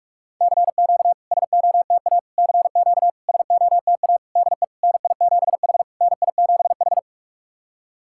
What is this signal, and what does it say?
Morse code, heard as a keyed tone, about 44 wpm, 690 Hz. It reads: CQ SOTA CQ SOTA DE NI7H NI7H
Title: SOTA QSO Practice 45wpm